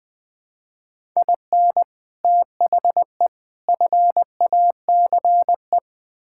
Morse code, heard as a keyed tone, about 20 wpm, 700 Hz. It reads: IN THE FACE
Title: Build A Phrase - 5-Word Phrases - 20wpm